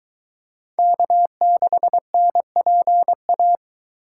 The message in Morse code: K6NPA

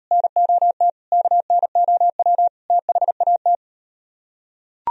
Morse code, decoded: NOT KNOW THAT

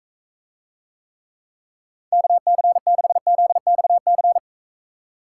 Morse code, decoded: KC6ZXC

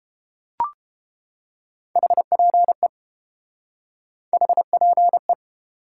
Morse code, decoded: HPE HPE

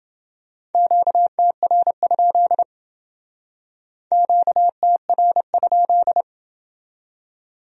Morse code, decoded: QTR? QTR?

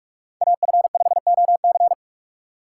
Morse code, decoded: AF5OC

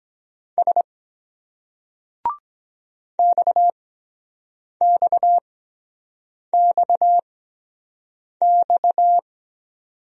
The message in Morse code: S XXXX

